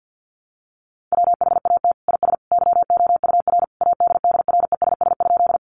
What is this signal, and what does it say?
Morse code, heard as a keyed tone, about 50 wpm, 700 Hz. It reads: WHAT IS YOUR ADDRESS?
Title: Sentences from Top 1000 Words 50wpm